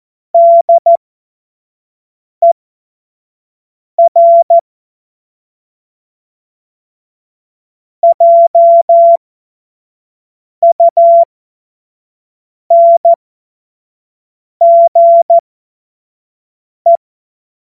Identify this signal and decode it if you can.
Morse code, heard as a keyed tone, about 14 wpm, spaced as for 5 wpm, 680 Hz. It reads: DER JUNGE